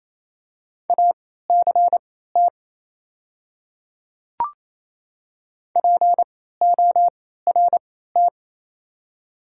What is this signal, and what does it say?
Morse code, heard as a keyed tone, about 28 wpm, spaced as for 16 wpm, 700 Hz. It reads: ACT PORT
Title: 3x Character Spacing - Top 200 Words - 28wpm